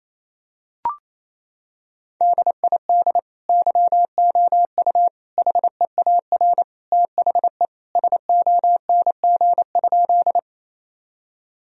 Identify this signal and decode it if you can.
Morse code, heard as a keyed tone, about 28 wpm, 700 Hz. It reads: DID YOU HEAR THE SONG?
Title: Sentences from Top 1000 Words - Review Part 2 - 28wpm